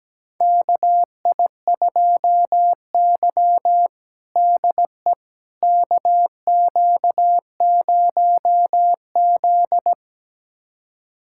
Morse code, heard as a keyed tone, about 17 wpm, 700 Hz. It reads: KI2Y DE KQ0Z
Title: USA Callsign DE Callsign - 4 Characters - 17wpm